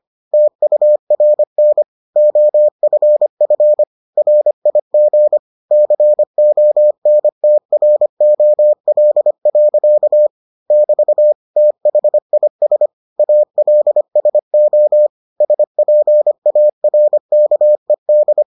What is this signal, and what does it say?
Morse code, heard as a keyed tone, about 25 wpm, 600 Hz. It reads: TURN OFF RIG CONTROL. <BT> THIS ALSO SPARKED